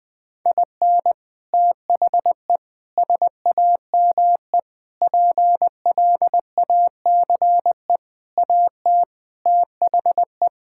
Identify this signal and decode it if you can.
Morse code, heard as a keyed tone, about 21 wpm, 700 Hz. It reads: IN THE SAME PLACE AT THE
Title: Sets of 8 Words - 20wpm